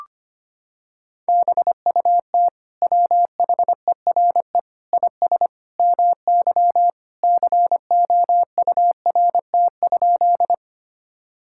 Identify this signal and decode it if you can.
Morse code, heard as a keyed tone, about 25 wpm, 700 Hz. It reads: BUT WHERE IS MY COURT?
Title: Sentences from Top 1600 Words 25wpm